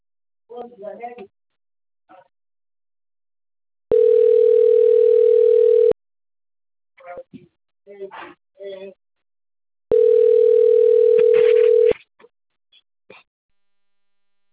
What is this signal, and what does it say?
Ringback tone (North America), 440 + 480 Hz, with speech.
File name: rg-1600-unknown- - - .26460